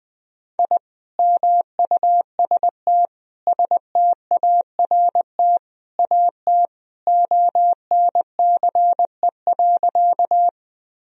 Morse code, heard as a keyed tone, about 20 wpm, 700 Hz. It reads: I MUST START AT ONCE.
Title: Sentences from Top 1000 Words - Review Part 1 - 20wpm